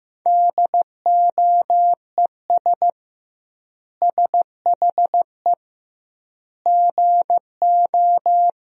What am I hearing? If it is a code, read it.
Morse code, heard as a keyed tone, about 15 wpm, 700 Hz. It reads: DOES SHE GO
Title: 2x Word-Spacing - Sentences from Top 400 Words - 15wpm